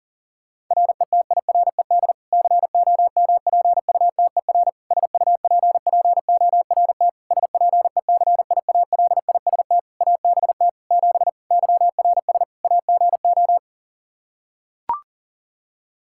Morse code, heard as a keyed tone, about 37 wpm, 700 Hz. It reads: RETIRED COMPUTER SUPPORT SPECIALIST ABT 7 YRS AGO
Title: Practice QSO - Exchange 3 - More Info - 40wpm